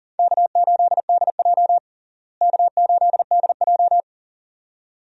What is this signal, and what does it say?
Morse code, heard as a keyed tone, about 40 wpm, 700 Hz. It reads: K8DJ K8DJ
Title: USA Call Signs - 4 Characters - Repeated 2x - 40wpm